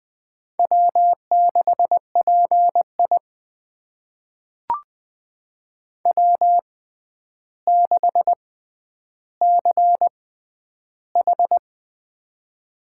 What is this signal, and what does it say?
Morse code, heard as a keyed tone, about 20 wpm, spaced as for 10 wpm, 700 Hz. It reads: W6PI W6CH
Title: USA Call Signs - 4 Characters - 6x Character Spacing - 20wpm